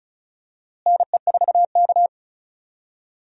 Morse code, heard as a keyed tone, about 36 wpm, 700 Hz. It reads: NE4K
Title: USA Call Signs - 4 Characters - Repeated 3x - 35wpm